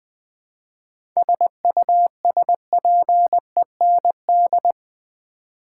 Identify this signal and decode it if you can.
Morse code, heard as a keyed tone, about 20 wpm, 700 Hz. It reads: SUSPEND
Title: 7 Letter Words - First 3 Letters Given - 20wpm